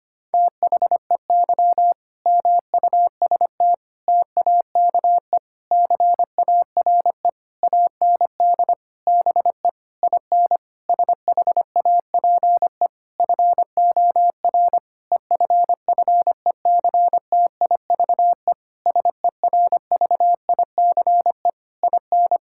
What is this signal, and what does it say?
Morse code, heard as a keyed tone, about 25 wpm, 700 Hz. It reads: THEY MUST TAKE CARE AND BE IN SHAPE FOR EFFECTIVE SERVICE IN
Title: Sentences from Top 1000 Words - Review Part 6 - 25wpm